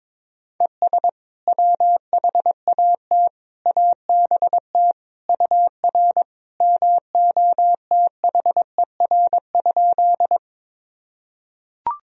Morse code, heard as a keyed tone, about 22 wpm, 700 Hz. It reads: ES WHAT ABT UR MOTHER?
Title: Sentences from Top 200 Words and Abbreviations 22wpm